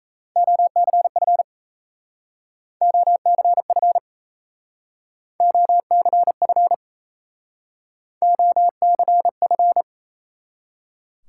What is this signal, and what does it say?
Morse code, heard as a keyed tone, about 36 wpm, 700 Hz. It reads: OCF OCF OCF OCF